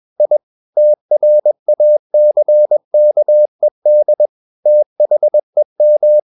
Morse code, heard as a keyed tone, about 21 wpm, 600 Hz. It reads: I TRACKED THEM
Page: CW med Gnister